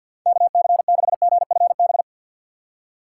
Morse code, heard as a keyed tone, about 48 wpm, 700 Hz. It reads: KC6GFB
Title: USA Call Signs - 6 Characters - 4x Character Spacing - 50wpm